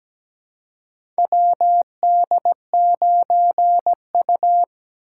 Morse code, heard as a keyed tone, about 17 wpm, 700 Hz. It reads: WD9U
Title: USA Call Signs - 4 Characters - 3x Character Spacing - 17wpm